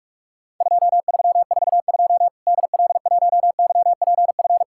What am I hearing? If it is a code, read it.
Morse code, heard as a keyed tone, about 48 wpm, 700 Hz. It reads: 2342 DL1YPF
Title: Worked All Europe DX Contest - QTC Exchange - 45wpm